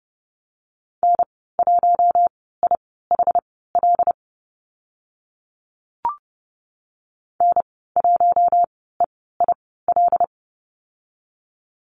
Morse code, heard as a keyed tone, about 30 wpm, spaced as for 17 wpm, 700 Hz. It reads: N1IHL N1EIL
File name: Lesson 14g - Number 2 - Call Signs 30wpm